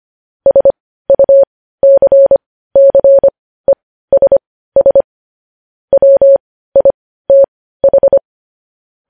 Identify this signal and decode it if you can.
Morse code, heard as a keyed tone, about 25 wpm, spaced as for 15 wpm, 550 Hz. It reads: SUCCESS WITH
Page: news-headlines-in-morse-code-at-15-wpm